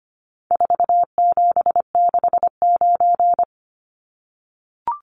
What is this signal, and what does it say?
Morse code, heard as a keyed tone, about 25 wpm, 700 Hz. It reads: 4769
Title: Numbers - 4-Digit 25wpm